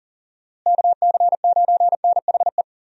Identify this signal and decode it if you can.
Morse code, heard as a keyed tone, about 40 wpm, 700 Hz. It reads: KC9NHE
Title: USA Call Signs - 6 Characters - Repeated 3x - 40wpm